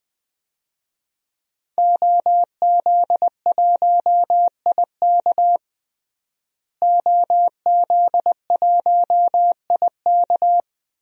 Morse code, heard as a keyed tone, about 20 wpm, 700 Hz. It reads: OZ1IK OZ1IK
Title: DXCC-100 Call Signs - 5 Characters - Repeated 2x - 20wpm